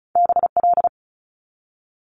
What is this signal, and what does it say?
Morse code, heard as a keyed tone, about 36 wpm, 700 Hz. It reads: BL
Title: Multi-letter Phonemes 35wpm